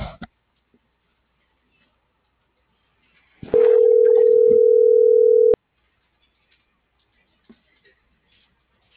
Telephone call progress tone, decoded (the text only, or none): ringback tone